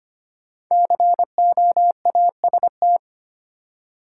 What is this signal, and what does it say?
Morse code, heard as a keyed tone, about 25 wpm, 700 Hz. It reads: COAST
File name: 5 Letter Words - First 3 Letters Given - 25wpm